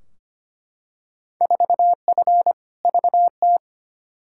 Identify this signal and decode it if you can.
Morse code, heard as a keyed tone, about 25 wpm, 700 Hz. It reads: 4F VT
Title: Field Day Exchanges Encore - 25wpm